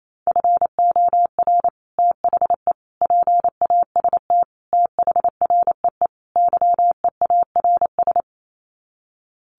Morse code, heard as a keyed tone, about 28 wpm, 700 Hz. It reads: FOR THE PAST THREE YEARS
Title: Sets of 5 Words Encore2 28wpm